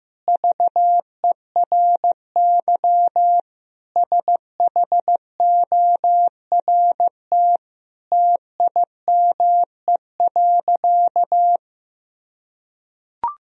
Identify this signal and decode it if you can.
Morse code, heard as a keyed tone, about 15 wpm, 700 Hz. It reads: VERY SHORT TIME.
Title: Sentences from Top 1000 Words - Review Part 3 - 15wpm